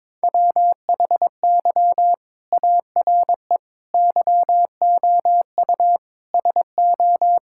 Morse code, heard as a keyed tone, about 22 wpm, 700 Hz. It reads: WHY ARE YOU SO
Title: Sentences from Top 1000 Words - Review Part 5 - 22wpm